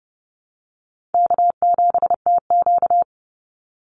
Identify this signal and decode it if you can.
Morse code, heard as a keyed tone, about 30 wpm, 700 Hz. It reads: K7TQ
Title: NA SOTA Chaser Callsigns 30wpm